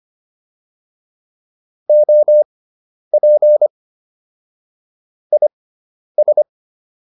Morse code, heard as a keyed tone, about 25 wpm, spaced as for 10 wpm, 600 Hz. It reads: OP IS
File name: QSO104 10